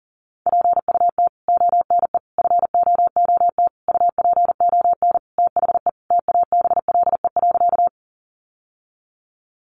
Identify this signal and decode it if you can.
Morse code, heard as a keyed tone, about 40 wpm, 700 Hz. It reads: PUT ONE FOOT UPON THE TABLE.
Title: Sentences from Top 1100 Words 40wpm